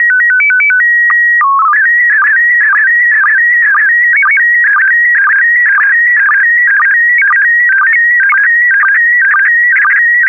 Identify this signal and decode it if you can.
SSTV (slow-scan television), PD 120.
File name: ISS-SSTV / SSTV-Audio-ISS-Bild-10